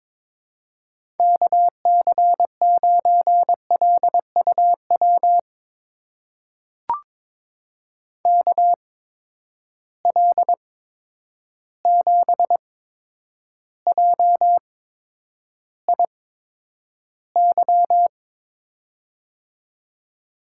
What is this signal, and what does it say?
Morse code, heard as a keyed tone, about 22 wpm, spaced as for 9 wpm, 700 Hz. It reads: KC9LUW KL7JIY
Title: USA Call Signs - 6 Characters - 8x Character Spacing - 22wpm